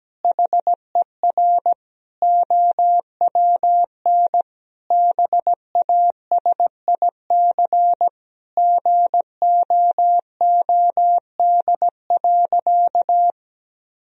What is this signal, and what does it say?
Morse code, heard as a keyed tone, about 17 wpm, 700 Hz. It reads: HER OWN BASIC GOOD.
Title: Sentences from Top 1000 Words - Review Part 3 - 17wpm